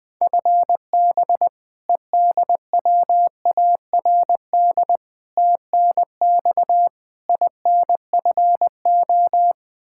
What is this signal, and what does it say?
Morse code, heard as a keyed tone, about 20 wpm, 700 Hz. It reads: FB EDWARD TNX INFO
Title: Practice QSO - Exchange 3 - More Info - 20wpm